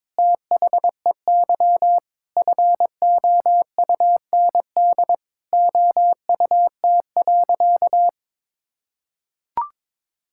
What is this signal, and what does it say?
Morse code, heard as a keyed tone, about 22 wpm, 700 Hz. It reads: THEY FOUND OUT.
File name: practice / 22wpm / Sentences with 8-Word Phrases - 22wpm